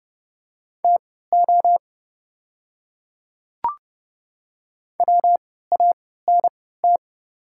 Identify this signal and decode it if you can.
Morse code, heard as a keyed tone, about 32 wpm, spaced as for 17 wpm, 700 Hz. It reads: TO WANT
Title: 3x Character Spacing - Top 200 Words - 30wpm